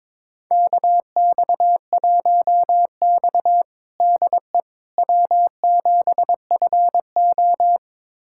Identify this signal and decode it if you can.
Morse code, heard as a keyed tone, about 22 wpm, 700 Hz. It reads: KX1X DE W7FO
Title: USA Callsign DE Callsign - 4 Characters - 22wpm